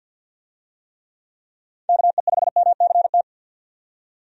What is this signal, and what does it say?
Morse code, heard as a keyed tone, about 52 wpm, 700 Hz. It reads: KE5MCT
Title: USA Call Signs - 6 Characters - 50wpm